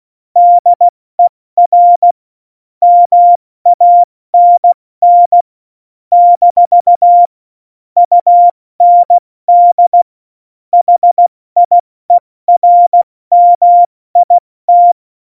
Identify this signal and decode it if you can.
Morse code, heard as a keyed tone, about 16 wpm, 700 Hz. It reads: DER MANN - UND HIERMIT
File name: Morsekurs / Witze / Witz 30. Juli 2019, 70 F80